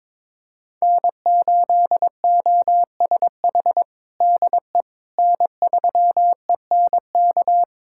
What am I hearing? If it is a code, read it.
Morse code, heard as a keyed tone, about 22 wpm, 700 Hz. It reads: N8OSH DE N3ENK
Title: USA Callsign DE Callsign - 5 Characters - 22wpm